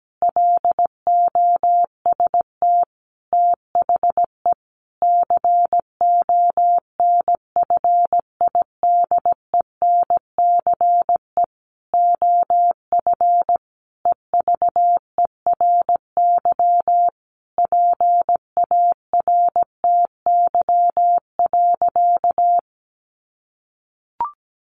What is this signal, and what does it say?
Morse code, heard as a keyed tone, about 17 wpm, 700 Hz. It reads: LOST THE CONFIDENCE OF EVERY PARTY.
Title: Sentences from Top 1500 Words 17wpm